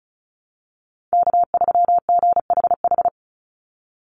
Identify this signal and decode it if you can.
Morse code, heard as a keyed tone, about 34 wpm, 700 Hz. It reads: K3GHH
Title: Call Signs Repeated 3x - 35wpm